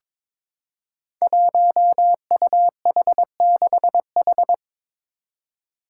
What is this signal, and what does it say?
Morse code, heard as a keyed tone, about 22 wpm, 700 Hz. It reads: 1UH6H
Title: Five Challenging Letter-Number Combinations 22wpm